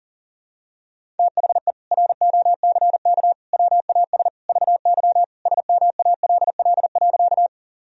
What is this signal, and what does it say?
Morse code, heard as a keyed tone, about 40 wpm, 700 Hz. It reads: THE ROCK WAS VY SMALL.
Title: Sentences from Top 400 Words and Abbreviations 40wpm